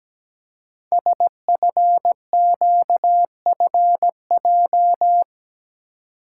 Morse code, heard as a keyed tone, about 17 wpm, 700 Hz. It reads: SFQFJ
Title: Code Groups - 5 Letters - 17wpm